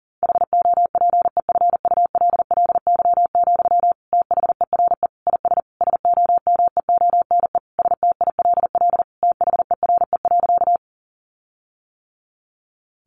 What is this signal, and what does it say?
Morse code, heard as a keyed tone, about 40 wpm, 700 Hz. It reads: HOPEFULLY, THERE IS SOMEONE STILL THERE.
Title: Sentences from Top 1500 Words 40wpm